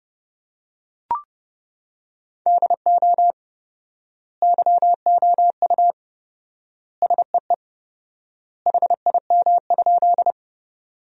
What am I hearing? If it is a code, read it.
Morse code, heard as a keyed tone, about 32 wpm, 700 Hz. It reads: DO YOU SEE HIM?
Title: 4x Word-Spacing - Sentences from Top 100 Words - 30wpm